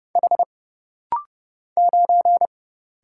Morse code, heard as a keyed tone, about 30 wpm, 700 Hz. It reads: H 9
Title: Single Letter-Number - Rapid-Fire 30wpm